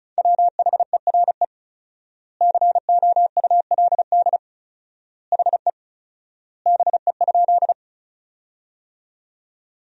Morse code, heard as a keyed tone, about 34 wpm, 700 Hz. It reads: WHERE COULD HE BE?
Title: 4x Word-Spacing - Sentences from Top 200 Words - 35wpm